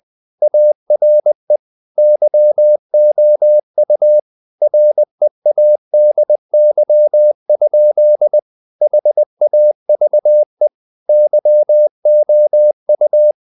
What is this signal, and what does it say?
Morse code, heard as a keyed tone, about 20 wpm, 600 Hz. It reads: ARE YOU READY? HAVE YOU
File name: the old man and the sea / the old man and the sea part 1 20wpm